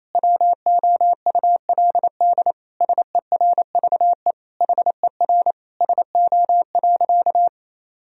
Morse code, heard as a keyed tone, about 28 wpm, 700 Hz. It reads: WOULD SERVE HER SO.
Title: Sentences from Top 1000 Words - Review Part 2 - 28wpm